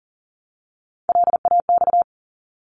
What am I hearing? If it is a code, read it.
Morse code, heard as a keyed tone, about 40 wpm, 700 Hz. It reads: LAX
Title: ARRL Sections 40wpm